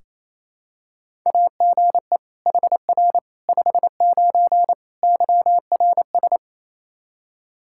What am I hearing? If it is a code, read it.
Morse code, heard as a keyed tone, about 27 wpm, 700 Hz. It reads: AGE HR 59 YRS
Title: K1USN SST QSO Practice 28wpm